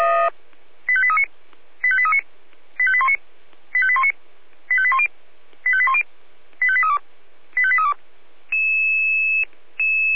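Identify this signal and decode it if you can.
Five-tone selective call (ZVEI), heard as quick runs of five tones, tones of about 70 ms. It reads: 85839 85839 85819 85819 85810 85810 85832 85832
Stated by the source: ZVEI five-tone call received by SDR.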